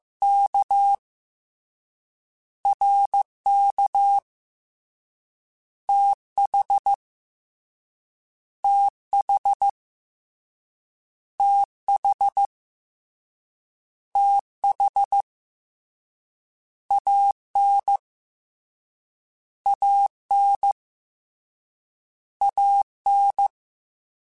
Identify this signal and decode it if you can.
Morse code, heard as a keyed tone, about 15 wpm, 800 Hz. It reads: K R K T H T H T H T H A N A N A N
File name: Contrasts / Contrasts-mp3s / 2LtrContrasts @13